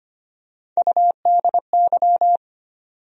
Morse code, heard as a keyed tone, about 25 wpm, 700 Hz. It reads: UDY
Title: USA Call Sign Postfixes - 3 Letters 25wpm